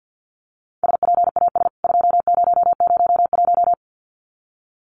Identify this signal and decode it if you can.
Morse code, heard as a keyed tone, about 50 wpm, 700 Hz. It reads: SPAS 2001
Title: CWT Exchange 50wpm